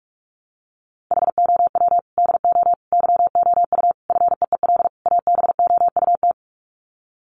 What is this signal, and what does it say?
Morse code, heard as a keyed tone, about 44 wpm, 700 Hz. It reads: HOW DO YOU FEEL ABOUT
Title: Sets of 5 Words 45wpm